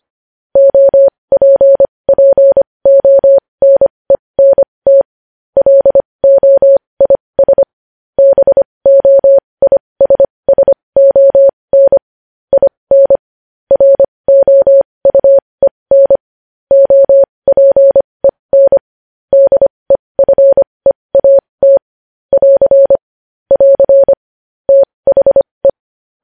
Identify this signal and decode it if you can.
Morse code, heard as a keyed tone, about 25 wpm, spaced as for 20 wpm, 550 Hz. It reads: OPPONENT LOIS BOISSON IN ROUEN OPEN DEFEAT <AR> <AR> THE
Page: news-headlines-in-morse-code-at-20-wpm